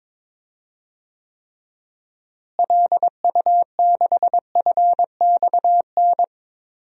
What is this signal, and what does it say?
Morse code, heard as a keyed tone, about 22 wpm, 700 Hz. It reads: LU6FXN
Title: DXCC-100 Call Signs - 6 Characters - Repeated 2x - 22wpm